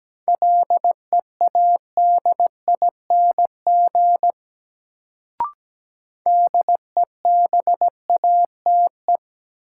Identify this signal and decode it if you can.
Morse code, heard as a keyed tone, about 17 wpm, 700 Hz. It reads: LEADING DEBATE
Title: Sentences from Top 1100 Words - New Word Drill - 17wpm